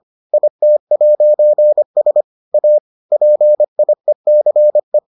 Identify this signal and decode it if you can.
Morse code, heard as a keyed tone, about 25 wpm, 600 Hz. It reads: IT'S A PIECE